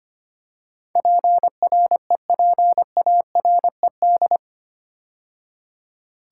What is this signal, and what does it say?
Morse code, heard as a keyed tone, about 25 wpm, 700 Hz. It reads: PREPARED
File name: 8 Letter Words - First 3 Letters Given - 25wpm